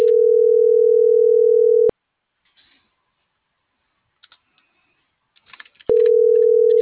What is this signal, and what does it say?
Ringback tone (North America), 440 + 480 Hz.